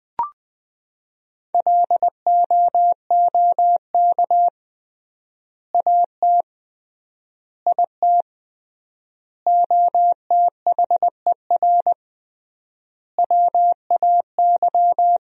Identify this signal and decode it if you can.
Morse code, heard as a keyed tone, about 20 wpm, 700 Hz. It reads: LOOK AT IT OTHER WAY
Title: 3x Word-Spacing - Sentences from Top 200 Words - 20wpm